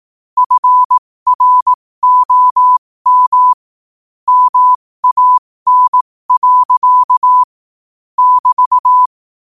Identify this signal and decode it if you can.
Morse code, heard as a keyed tone, about 18 wpm, spaced as for 15 wpm, 1000 Hz. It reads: FROM MAN. <BT>
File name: qotdfiles / 15wpm